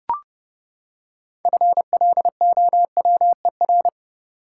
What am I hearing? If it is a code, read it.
Morse code, heard as a keyed tone, about 30 wpm, 700 Hz. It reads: FLOWER
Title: Six Letter Words - 30wpm